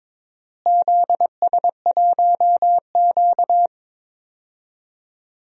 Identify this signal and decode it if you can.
Morse code, heard as a keyed tone, about 22 wpm, 700 Hz. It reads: ZS1Q
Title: DXCC-100 Call Signs - 4 Characters - Repeated 2x - 22wpm